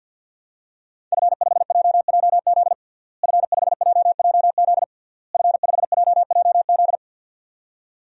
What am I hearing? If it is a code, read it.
Morse code, heard as a keyed tone, about 51 wpm, 700 Hz. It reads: F5JJZ F5JJZ F5JJZ
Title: DXCC-100 Call Signs - 5 Characters - Repeated 3x - 50wpm